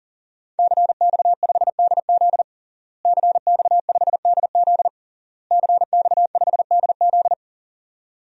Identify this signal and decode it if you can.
Morse code, heard as a keyed tone, about 40 wpm, 700 Hz. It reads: CX5DZ CX5DZ CX5DZ
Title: DXCC-100 Call Signs - 5 Characters - Repeated 3x - 40wpm